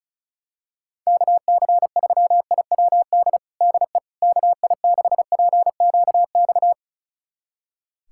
Morse code, heard as a keyed tone, about 36 wpm, 690 Hz. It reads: KC3IWD DE KI6PQX
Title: USA Callsign DE Callsign - 6 Characters - 35wpm